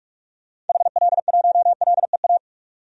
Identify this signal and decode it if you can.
Morse code, heard as a keyed tone, about 42 wpm, 700 Hz. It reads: HL1LEA
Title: DXCC-100 Call Signs - 6 Characters - 45wpm